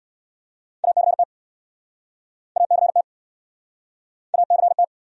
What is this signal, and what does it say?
Morse code, heard as a keyed tone, about 71 wpm, 700 Hz. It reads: ACT ACT ACT